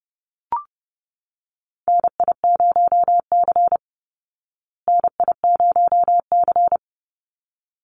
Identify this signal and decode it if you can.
Morse code, heard as a keyed tone, about 30 wpm, 700 Hz. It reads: NI0C NI0C
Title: Call Signs Repeated 3x - 30wpm